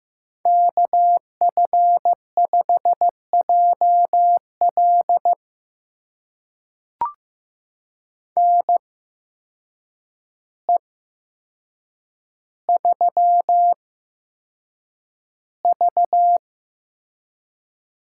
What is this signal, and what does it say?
Morse code, heard as a keyed tone, about 15 wpm, spaced as for 7 wpm, 700 Hz. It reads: KF5JL NE3V
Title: USA Call Signs - 8x Character Spacing - 15wpm